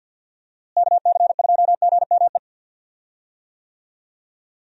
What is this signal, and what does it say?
Morse code, heard as a keyed tone, about 45 wpm, 700 Hz. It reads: KC2GME
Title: USA Call Signs - 6 Characters - Repeated 2x - 50wpm